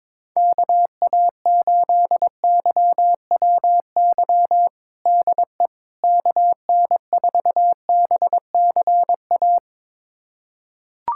Morse code, heard as a keyed tone, about 22 wpm, 700 Hz. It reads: KA8YWY DE KN4BCA
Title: USA Callsign DE Callsign - 6 Characters - 22wpm